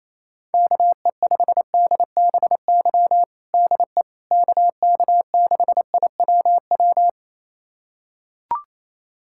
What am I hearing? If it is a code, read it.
Morse code, heard as a keyed tone, about 28 wpm, 700 Hz. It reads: KE5DBY DE KK6IWW
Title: USA Callsign DE Callsign - 6 Characters - 28wpm